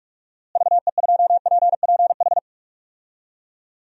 Morse code, heard as a keyed tone, about 41 wpm, 700 Hz. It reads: VE2PPH
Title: DXCC-100 Call Signs - 6 Characters - Repeated 3x - 45wpm